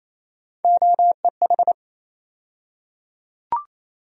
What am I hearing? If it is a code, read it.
Morse code, heard as a keyed tone, about 28 wpm, 700 Hz. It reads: OEH
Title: USA Call Sign Postfixes - 3 Letters 28wpm